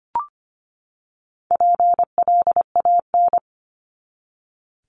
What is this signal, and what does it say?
Morse code, heard as a keyed tone, about 25 wpm, 700 Hz. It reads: PLAN
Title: ING Suffix - Top 100 Words with Base Words 25wpm